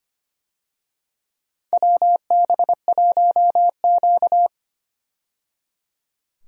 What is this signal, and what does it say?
Morse code, heard as a keyed tone, about 25 wpm, 700 Hz. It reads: WB1Q
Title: USA Call Signs - 4 Characters - 25wpm